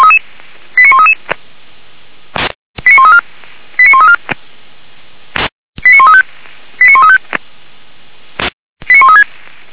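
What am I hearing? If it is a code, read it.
Five-tone selective call (ZVEI), heard as quick runs of five tones, tones of about 70 ms. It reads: 89133 89134 89134 89135 89135 89136
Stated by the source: ZVEI five-tone call 89132 received by SDR.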